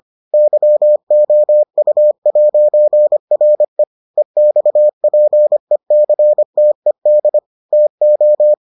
Morse code, heard as a keyed tone, about 25 wpm, 600 Hz. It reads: YOU'RE EXPECTED TO